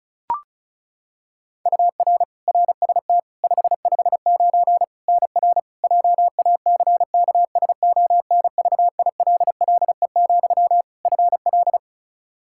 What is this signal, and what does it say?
Morse code, heard as a keyed tone, about 34 wpm, 700 Hz. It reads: UR RST 559 NR JACKSONVILLE, FL
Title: Practice QSO - Exchange 1 - Beginning - 35wpm